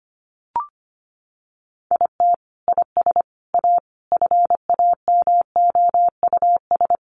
Morse code, heard as a keyed tone, about 25 wpm, 700 Hz. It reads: IT IS A FAMOUS
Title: Sentences from Top 1500 Words 25wpm